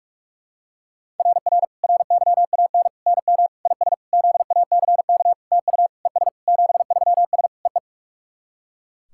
Morse code, heard as a keyed tone, about 44 wpm, 690 Hz. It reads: RR RYAN NM IS ZACK TU ES 73S EE
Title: SOTA QSO Practice 45wpm